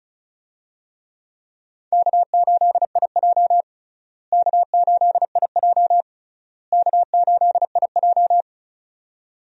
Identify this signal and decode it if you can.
Morse code, heard as a keyed tone, about 36 wpm, 700 Hz. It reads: K8IJ K8IJ K8IJ
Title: USA Call Signs - 4 Characters - Repeated 3x - 35wpm